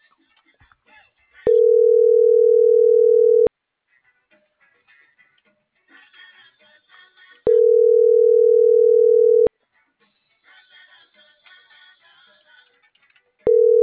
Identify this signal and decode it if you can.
Ringback tone (North America), 440 + 480 Hz.